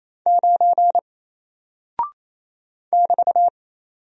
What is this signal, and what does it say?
Morse code, heard as a keyed tone, about 27 wpm, 700 Hz. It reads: 9 <BT>
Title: Single Character (Letter:Number:Punctuation:Prosign) 28wpm